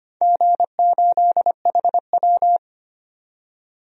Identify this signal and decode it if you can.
Morse code, heard as a keyed tone, about 25 wpm, 700 Hz. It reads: G8HW
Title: DXCC-100 Call Signs - 4 Characters - 25wpm